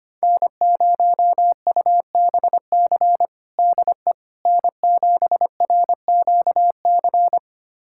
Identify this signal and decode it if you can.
Morse code, heard as a keyed tone, about 25 wpm, 700 Hz. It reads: N0UBC DE N7RQC
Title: USA Callsign DE Callsign - 5 Characters - 25wpm